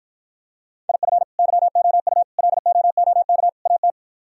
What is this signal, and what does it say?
Morse code, heard as a keyed tone, about 53 wpm, 700 Hz. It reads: IF YOU LOOK AT